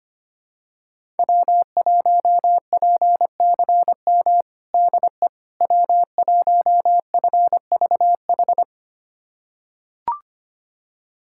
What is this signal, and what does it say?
Morse code, heard as a keyed tone, about 25 wpm, 700 Hz. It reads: W1PCM DE W1FVH
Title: USA Callsign DE Callsign - 5 Characters - 25wpm